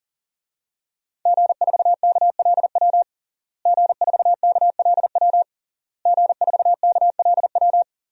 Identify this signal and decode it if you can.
Morse code, heard as a keyed tone, about 40 wpm, 700 Hz. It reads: G4KLW G4KLW G4KLW
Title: DXCC-100 Call Signs - 5 Characters - Repeated 3x - 40wpm